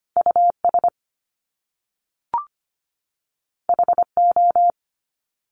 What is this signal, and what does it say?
Morse code, heard as a keyed tone, about 25 wpm, 700 Hz. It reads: US HO
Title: Top Two Letter Combinations in Words 25wpm